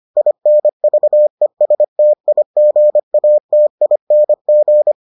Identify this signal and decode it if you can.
Morse code, heard as a keyed tone, about 25 wpm, 600 Hz. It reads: INVESTIGATING